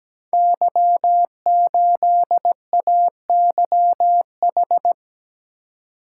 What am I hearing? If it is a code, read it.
Morse code, heard as a keyed tone, about 17 wpm, 700 Hz. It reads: Y8AYH
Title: Code Groups - 5 Letter:Numbers - 17wpm